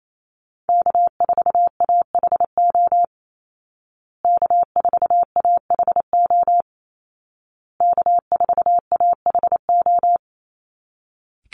Morse code, heard as a keyed tone, about 28 wpm, 700 Hz. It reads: K4AHO K4AHO K4AHO
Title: Call Signs Repeated 2x - 28wpm